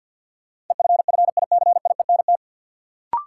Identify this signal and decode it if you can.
Morse code, heard as a keyed tone, about 53 wpm, 700 Hz. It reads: EFFICIENT
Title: Sentences from Top 1300 Words - New Word Drill Alt - 50wpm